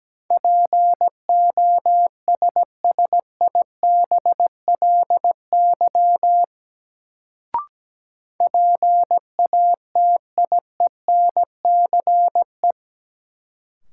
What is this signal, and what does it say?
Morse code, heard as a keyed tone, about 17 wpm, 700 Hz. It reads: POSSIBLY PATIENCE
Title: Sentences from Top 1300 Words - New Word Drill - 17wpm